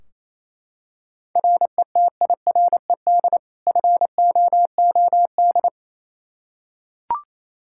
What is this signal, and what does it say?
Morse code, heard as a keyed tone, about 28 wpm, 700 Hz. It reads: RETIRED FOOD
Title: Practice QSO - Exchange 3 - More Info - 28wpm